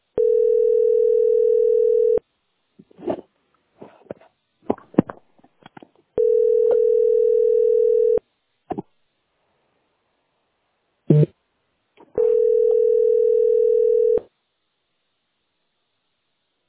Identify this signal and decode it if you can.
Ringback tone (North America), 440 + 480 Hz.